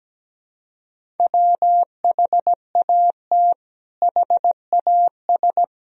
Morse code, heard as a keyed tone, about 17 wpm, 700 Hz. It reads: WHAT HAS
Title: Sentences from Top 1000 Words - Review Part 2 - 17wpm